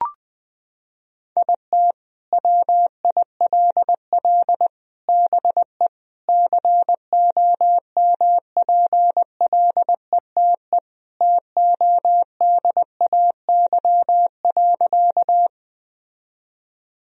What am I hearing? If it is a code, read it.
Morse code, heard as a keyed tone, about 20 wpm, 700 Hz. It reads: IT WILL BE COMPLETE TODAY.
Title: Sentences from Top 1000 Words - Review Part 2 - 20wpm